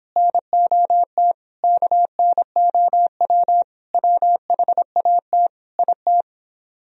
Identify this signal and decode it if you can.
Morse code, heard as a keyed tone, about 26 wpm, 700 Hz. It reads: NOT KNOW WHAT IT